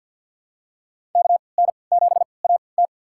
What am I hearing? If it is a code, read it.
Morse code, heard as a keyed tone, about 53 wpm, spaced as for 29 wpm, 700 Hz. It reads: KN7AT
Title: USA Call Signs - 3x Character Spacing - 50wpm